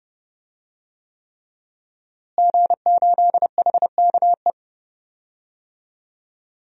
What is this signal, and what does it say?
Morse code, heard as a keyed tone, about 31 wpm, 700 Hz. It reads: G8HKE
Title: DXCC-100 Call Signs - 5 Characters - Repeated 3x - 30wpm